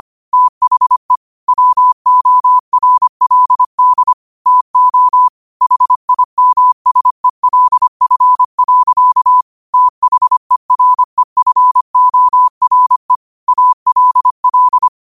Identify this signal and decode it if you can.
Morse code, heard as a keyed tone, about 25 wpm, 1000 Hz. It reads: THE WORLD TO HIMSELF. THEREFORE ALL
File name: qotdfiles / 25wpm